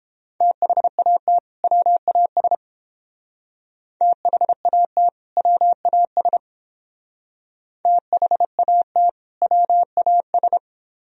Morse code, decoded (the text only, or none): THAT WAS THAT WAS THAT WAS